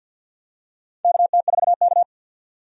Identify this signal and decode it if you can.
Morse code, heard as a keyed tone, about 52 wpm, 700 Hz. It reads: KT4K
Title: USA Call Signs - 4 Characters - 50wpm